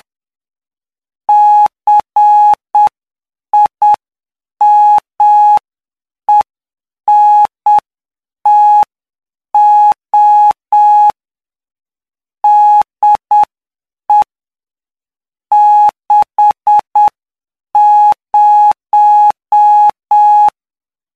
Morse code, heard as a keyed tone, about 7 wpm, 840 Hz. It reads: CIMENTO DE 60
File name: cw / py2lcd / 16-PY2LCD-Li��o 25